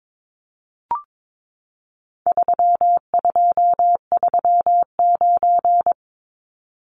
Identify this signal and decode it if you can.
Morse code, heard as a keyed tone, about 22 wpm, 700 Hz. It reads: 3239
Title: Numbers - 4-Digit 22wpm